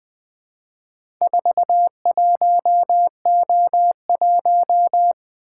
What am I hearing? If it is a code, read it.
Morse code, heard as a keyed tone, about 20 wpm, 700 Hz. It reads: 41O1
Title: Four Letter-Number Combinations - 20wpm